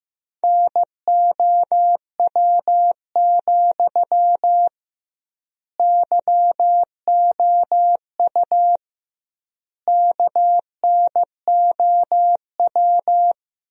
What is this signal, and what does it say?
Morse code, heard as a keyed tone, about 15 wpm, 700 Hz. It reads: NOW, YOU KNOW
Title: 2x Word-Spacing - Sentences from Top 200 Words - 15wpm